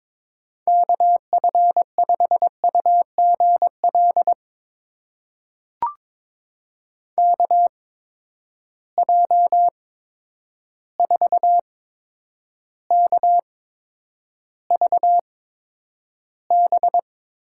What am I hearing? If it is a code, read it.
Morse code, heard as a keyed tone, about 21 wpm, spaced as for 9 wpm, 700 Hz. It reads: KF5UGL KJ4KVB
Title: USA Call Signs - 6 Characters - 8x Character Spacing - 22wpm